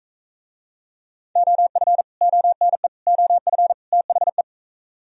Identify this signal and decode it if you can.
Morse code, heard as a keyed tone, about 42 wpm, 700 Hz. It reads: OF ONE OF THE